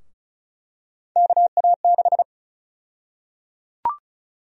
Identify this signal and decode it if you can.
Morse code, heard as a keyed tone, about 34 wpm, 700 Hz. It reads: KA6
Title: USA Call Sign Prefixes - Repeated 2x - 35wpm